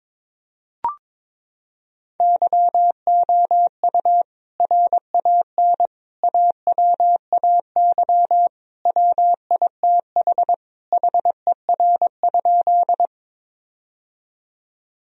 Morse code, heard as a keyed tone, about 22 wpm, 700 Hz. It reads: YOU RAN AWAY WITH HER?
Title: Sentences from Top 1000 Words - Review Part 3 - 22wpm